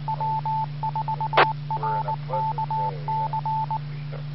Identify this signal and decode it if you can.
Morse code, heard as a keyed tone, about 19 wpm, 880 Hz. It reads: W4RKC
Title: Bluemont, VA Skywarn Repeater - WA4TSC